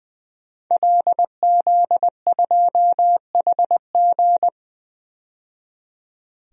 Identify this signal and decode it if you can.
Morse code, heard as a keyed tone, about 20 wpm, 700 Hz. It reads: LZ2HG
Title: DXCC-100 Call Signs - 5 Characters - 20wpm